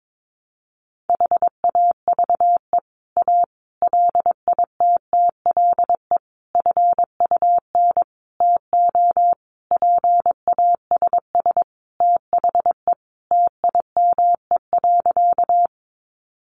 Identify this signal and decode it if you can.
Morse code, heard as a keyed tone, about 22 wpm, 700 Hz. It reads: HAVE A LITTLE FUN TO PASS THE TIME.
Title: Sentences from Top 700 Words 22wpm